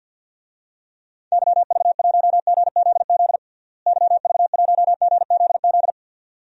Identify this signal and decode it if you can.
Morse code, heard as a keyed tone, about 47 wpm, 700 Hz. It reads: YV1GZZ YV1GZZ
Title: DXCC-100 Call Signs - 6 Characters - Repeated 2x - 50wpm